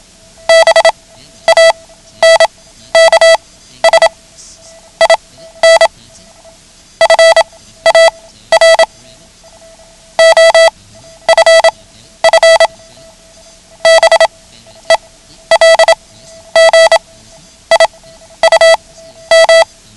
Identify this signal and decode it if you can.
Morse code, heard as a keyed tone, about 27 wpm, spaced as for 12 wpm, 660 Hz. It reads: BANKS IN FAR OFF BELGIUM
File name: ham radio / CodeQuick / Code Quick 09